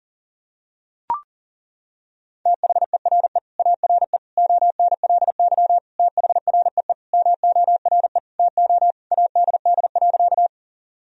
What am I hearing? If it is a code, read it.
Morse code, heard as a keyed tone, about 40 wpm, 700 Hz. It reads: THERE ARE ONLY THREE MORE TO ADD.
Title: Sentences from Top 1000 Words - Review Part 1 - 40wpm